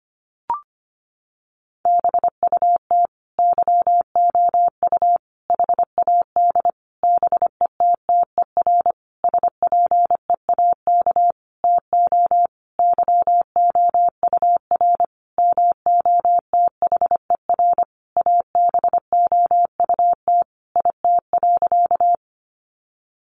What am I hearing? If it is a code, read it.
Morse code, heard as a keyed tone, about 25 wpm, 700 Hz. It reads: BUT YOU HAD BETTER SPEAK TO YOUR MOTHER ABOUT IT.
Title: Sentences from Top 1100 Words 25wpm